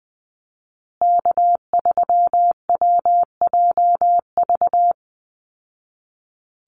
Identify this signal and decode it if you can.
Morse code, heard as a keyed tone, about 20 wpm, 700 Hz. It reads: K3WJV
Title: Call Signs Repeated 3x - 20wpm